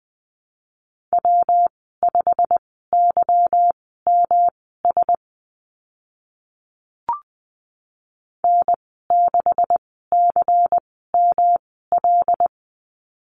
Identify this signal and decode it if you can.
Morse code, heard as a keyed tone, about 20 wpm, spaced as for 15 wpm, 700 Hz. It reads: W5YMS N6CML
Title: 147 Lesson 24g - Letter y - Call Signs 20wpm